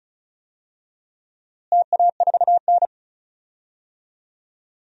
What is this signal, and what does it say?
Morse code, heard as a keyed tone, about 36 wpm, 700 Hz. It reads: TA4N
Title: DXCC-100 Call Signs - 4 Characters - Repeated 3x - 35wpm